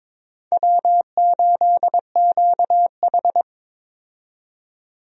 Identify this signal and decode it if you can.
Morse code, heard as a keyed tone, about 22 wpm, 700 Hz. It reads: W8QH
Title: USA Call Signs - 4 Characters - 22wpm